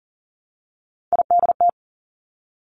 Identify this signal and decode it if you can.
Morse code, heard as a keyed tone, about 38 wpm, 700 Hz. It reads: IDT
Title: Three Challenging Letter-Number Combinations 40wpm